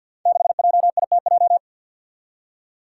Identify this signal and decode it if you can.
Morse code, heard as a keyed tone, about 52 wpm, 700 Hz. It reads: 6JITJ
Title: Five Challenging Letter-Number Combinations 50wpm